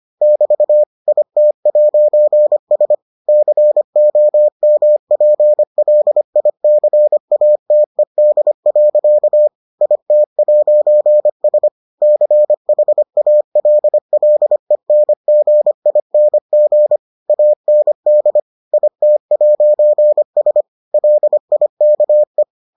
Morse code, heard as a keyed tone, about 25 wpm, 600 Hz. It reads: <BT> IT'S COMPLICATED. IT'S CHALLENGING AND IT'S LIKE